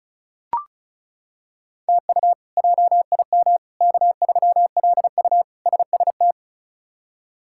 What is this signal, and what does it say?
Morse code, heard as a keyed tone, about 34 wpm, 700 Hz. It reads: TU JIM K3LU SST
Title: K1USN SST QSO Practice - 35wpm